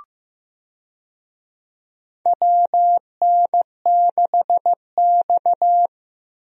Morse code, heard as a keyed tone, about 15 wpm, 700 Hz. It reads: WN6X
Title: USA Call Signs - 4 Characters - Repeated 3x - 15wpm